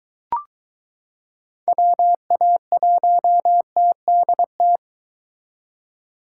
Morse code, heard as a keyed tone, about 22 wpm, 700 Hz. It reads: WA1TDT